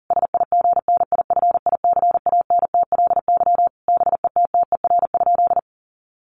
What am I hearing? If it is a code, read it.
Morse code, heard as a keyed tone, about 40 wpm, 700 Hz. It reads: SIGNIFICANTLY BETTER?
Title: Sentences from Top 1500 Words 40wpm